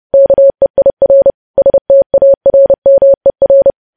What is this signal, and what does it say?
Morse code, heard as a keyed tone, about 30 wpm, 550 Hz. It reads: KEIR STARMER
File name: podcast / News headlines-1x-30x30